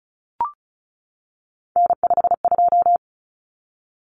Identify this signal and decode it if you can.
Morse code, heard as a keyed tone, about 34 wpm, 700 Hz. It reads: N52
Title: Common Cut Numbers - 3-Digit 35wpm